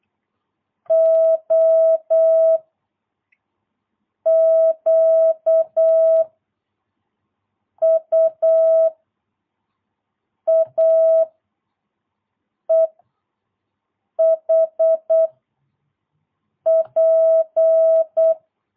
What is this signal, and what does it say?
Morse code, heard as a keyed tone, about 8 wpm, 650 Hz. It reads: O Q U A E H P
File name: jh8cbh / 2022cw